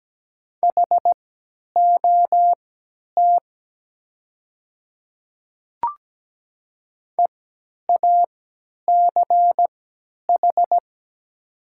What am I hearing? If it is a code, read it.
Morse code, heard as a keyed tone, about 17 wpm, spaced as for 10 wpm, 690 Hz. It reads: HOT EACH
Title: 3x Character Spacing - Top 100 Words - 17wpm